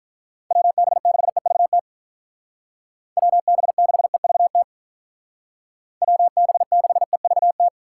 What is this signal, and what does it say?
Morse code, heard as a keyed tone, about 47 wpm, 700 Hz. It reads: WB6EVT WB6EVT WB6EVT